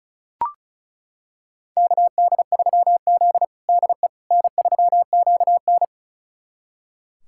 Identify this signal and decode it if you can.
Morse code, heard as a keyed tone, about 34 wpm, 700 Hz. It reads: KD3Z DE N3QN
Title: USA Callsign DE Callsign - 4 Characters - 35wpm